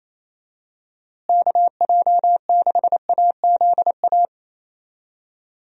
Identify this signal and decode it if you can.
Morse code, heard as a keyed tone, about 28 wpm, 700 Hz. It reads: KJ6AZA
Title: USA Call Signs - 6 Characters - Repeated 3x - 28wpm